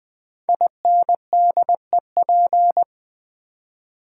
Morse code, heard as a keyed tone, about 20 wpm, 700 Hz. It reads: INDEP
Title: Build A Word - 5 Letters At A Time - 12-Letter Words - 20wpm